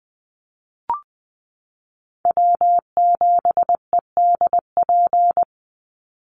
Morse code, heard as a keyed tone, about 20 wpm, 700 Hz. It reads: W7EDP
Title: USA Call Signs 20wpm